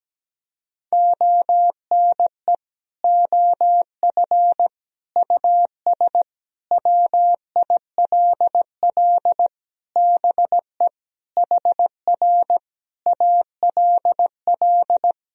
Morse code, heard as a keyed tone, about 17 wpm, 700 Hz. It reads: ONE OF US WILL BE HR ALL
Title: Sentences from Top 300 Words and Abbreviations 17wpm